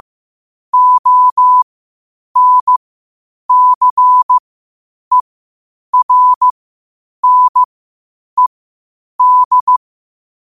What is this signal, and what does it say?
Morse code, heard as a keyed tone, about 15 wpm, spaced as for 8 wpm, 1000 Hz. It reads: ONCERNED
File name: qotdfiles / 8wpm